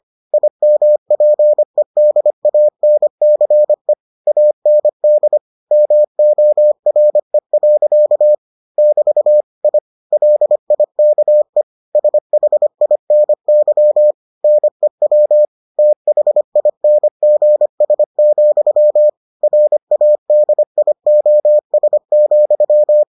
Morse code, heard as a keyed tone, about 25 wpm, 600 Hz. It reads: IMPEDANCE AND MORE. <BT> I LIKE SHINY NEW THINGS, RADIOS,